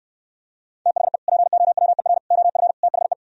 Morse code, heard as a keyed tone, about 67 wpm, 700 Hz. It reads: THE POWER OF THE